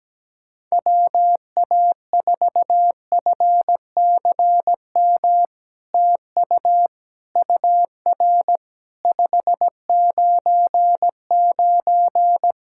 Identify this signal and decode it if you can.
Morse code, heard as a keyed tone, about 17 wpm, 700 Hz. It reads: WA4FCM TU UR 599
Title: SOTA QSO Practice 17wpm